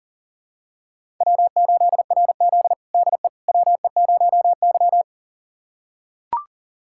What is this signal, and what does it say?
Morse code, heard as a keyed tone, about 40 wpm, 700 Hz. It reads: W8RZ DE WE0Y
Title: USA Callsign DE Callsign - 4 Characters - 40wpm